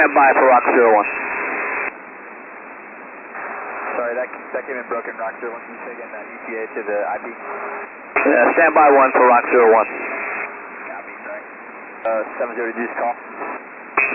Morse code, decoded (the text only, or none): none